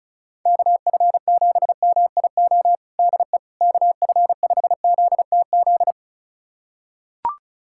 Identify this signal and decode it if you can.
Morse code, heard as a keyed tone, about 36 wpm, 700 Hz. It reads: KF7MIO DE KF5ZTZ
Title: USA Callsign DE Callsign 35wpm